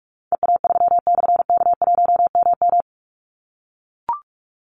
Morse code, heard as a keyed tone, about 44 wpm, 700 Hz. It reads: EA3/K1MM
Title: Call Signs - 45wpm